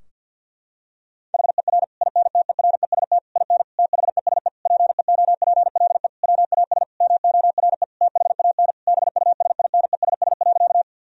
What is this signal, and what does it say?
Morse code, heard as a keyed tone, about 50 wpm, 700 Hz. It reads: HER INTEREST IN THESE PEOPLE WAS MORE THAN BUSINESS.
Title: Sentences from Top 1000 Words - Review Part 2 - 50wpm